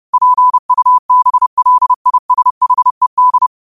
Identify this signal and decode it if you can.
Morse code, heard as a keyed tone, about 30 wpm, 1000 Hz. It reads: PUBLISHED